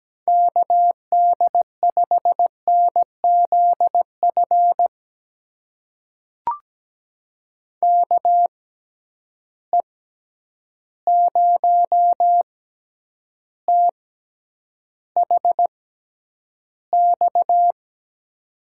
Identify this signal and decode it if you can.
Morse code, heard as a keyed tone, about 17 wpm, spaced as for 9 wpm, 700 Hz. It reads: KD5NZF KE0THX
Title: USA Call Signs - 6 Characters - 6x Character Spacing - 17wpm